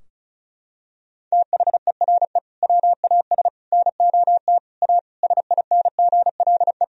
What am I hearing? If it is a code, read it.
Morse code, heard as a keyed tone, about 34 wpm, 700 Hz. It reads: THERE WAS NOT A SINGLE
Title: Sentences from Top 1000 Words - Review Part 4 - 35wpm